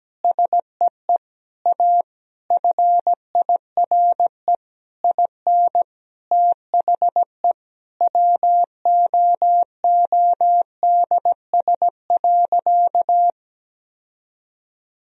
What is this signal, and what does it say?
Morse code, heard as a keyed tone, about 17 wpm, 700 Hz. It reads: SEE A FIRE IN THE WOODS.
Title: Sentences from Top 1000 Words - Review Part 2 - 17wpm